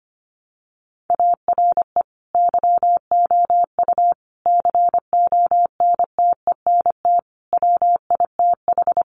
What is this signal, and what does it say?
Morse code, heard as a keyed tone, about 25 wpm, 700 Hz. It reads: ARE YOU CONTENT WITH
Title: Sentences from Top 900 Words 25wpm